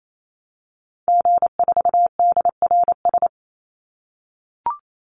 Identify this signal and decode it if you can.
Morse code, heard as a keyed tone, about 28 wpm, 700 Hz. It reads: G4DRS
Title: Call Signs - 28wpm